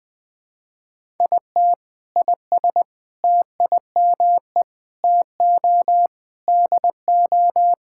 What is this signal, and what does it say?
Morse code, heard as a keyed tone, about 21 wpm, 700 Hz. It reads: IT IS TIME TO DO
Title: Sentences with 5-Word Phrases - 20wpm